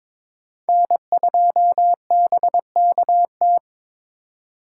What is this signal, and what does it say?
Morse code, heard as a keyed tone, about 22 wpm, 700 Hz. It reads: N2BKT
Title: USA Call Signs - 5 Characters - 2x Character Spacing - 22wpm